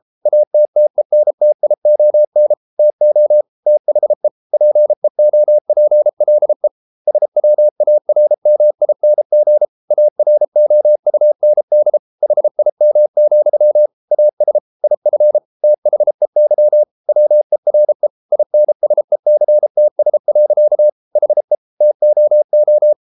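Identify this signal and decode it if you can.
Morse code, heard as a keyed tone, about 33 wpm, 600 Hz. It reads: ATTENTION TO THE PEOPLE SWARMING AROUND HIM, AS IF THEY WERE INSECTS. HE TOO
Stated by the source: Morse code practice at 32 wpm from CW med Gnister, reading Ken Follett's novel Never in English.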